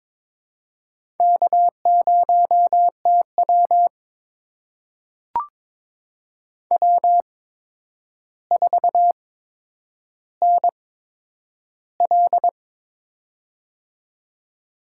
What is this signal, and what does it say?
Morse code, heard as a keyed tone, about 22 wpm, spaced as for 9 wpm, 700 Hz. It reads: K0TW W4NL
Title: USA Call Signs - 4 Characters - 8x Character Spacing - 22wpm